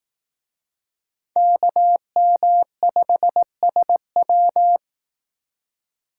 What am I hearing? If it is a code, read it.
Morse code, heard as a keyed tone, about 18 wpm, 700 Hz. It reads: KM5SW
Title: Speed Racing - USA 2x2 Call Signs 15wpm